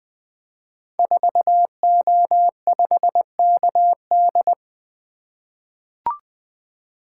Morse code, heard as a keyed tone, about 20 wpm, 700 Hz. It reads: 4O5KD
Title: Code Groups - 5 Letter:Numbers - 20wpm